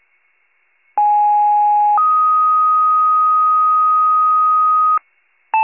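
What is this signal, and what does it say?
Two-tone page, heard as a one-second tone, then a three-second tone, about 830 then 1250 Hz. It is heard from 1.0 to 5.0 s.
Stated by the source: Fire department two-tone paging, county-wide paging archive.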